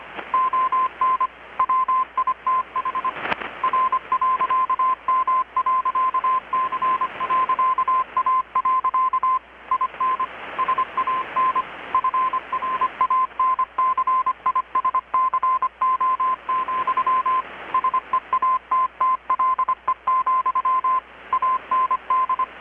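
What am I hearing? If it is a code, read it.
Morse code, heard as a keyed tone, about 25 wpm, 1040 Hz. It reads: ON WITH R.M.C.A. IN SAN FRANCISCO, SEATTLE, AND
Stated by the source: Morse from KPH's Night of Nights XX maritime radiotelegraphy event, received in Chiapas, Mexico about 3,800 km away on a Kenwood R-600 with a grounded 10 m long-wire antenna.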